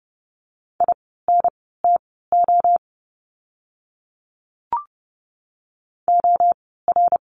Morse code, heard as a keyed tone, about 30 wpm, spaced as for 17 wpm, 700 Hz. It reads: INTO OR
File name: course / 30:10wpm / 108 Lesson 18f - Letter m - Words 30wpm